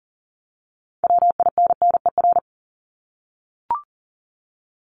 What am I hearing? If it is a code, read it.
Morse code, heard as a keyed tone, about 40 wpm, 700 Hz. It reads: WINNER
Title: Top 1751-2000 Words 40wpm